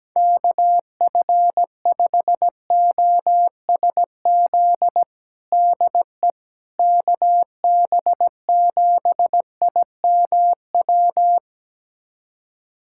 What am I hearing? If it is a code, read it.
Morse code, heard as a keyed tone, about 17 wpm, 700 Hz. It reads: KF5OSZ DE KB7IMW
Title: USA Callsign DE Callsign 17wpm